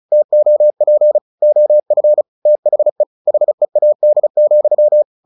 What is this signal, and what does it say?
Morse code, heard as a keyed tone, about 35 wpm, 600 Hz. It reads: TOP OF THE HEAD,
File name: the old man and the sea 2 35wpm